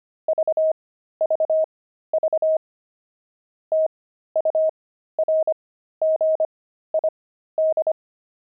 Morse code, heard as a keyed tone, about 25 wpm, spaced as for 13 wpm, 630 Hz. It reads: VVV TURGID
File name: pangram-fs-13-cs-25-saxophones